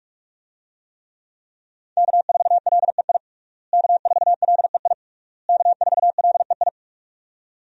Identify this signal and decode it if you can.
Morse code, heard as a keyed tone, about 46 wpm, 700 Hz. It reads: K4LEI K4LEI K4LEI
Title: USA Call Signs - 5 Characters - Repeated 3x - 45wpm